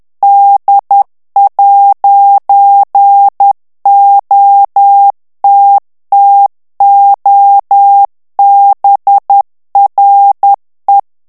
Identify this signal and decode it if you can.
Morse code, heard as a keyed tone, about 10 wpm, 800 Hz. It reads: D'OTTOBRE
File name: ITALIEN texte clair 1 vitesse 10 mots minute